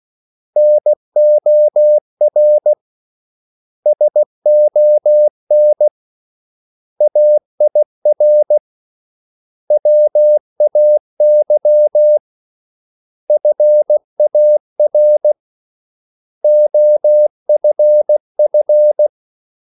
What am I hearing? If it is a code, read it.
Morse code, heard as a keyed tone, about 16 wpm, 600 Hz. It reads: NOR SON AIR WAY FAR OFF
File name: CW / Practice / 500MostCommonEnglishWords16